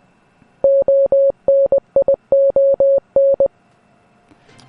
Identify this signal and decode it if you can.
Morse code, heard as a keyed tone, about 20 wpm, 550 Hz. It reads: ONION